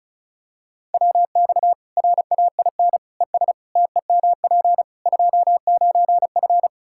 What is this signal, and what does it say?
Morse code, heard as a keyed tone, about 34 wpm, 700 Hz. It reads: WX RAIN ES TEMP 29F
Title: Practice QSO - Exchange 2 - Info - 35wpm